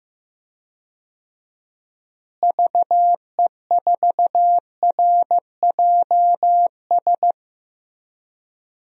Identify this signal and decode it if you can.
Morse code, heard as a keyed tone, about 15 wpm, 700 Hz. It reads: VE4RJS
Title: DXCC-100 Call Signs - 6 Characters - Repeated 3x - 15wpm